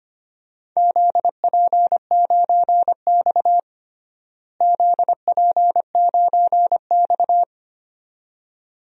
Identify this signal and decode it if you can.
Morse code, heard as a keyed tone, about 25 wpm, 700 Hz. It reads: ZP9X ZP9X
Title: DXCC-100 Call Signs - 4 Characters - Repeated 2x - 25wpm